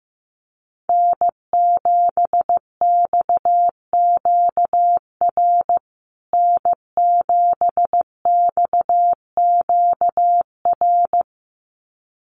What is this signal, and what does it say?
Morse code, heard as a keyed tone, about 15 wpm, 700 Hz. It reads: N7XQR N7XQR
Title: USA Call Signs Repeated 2x 15wpm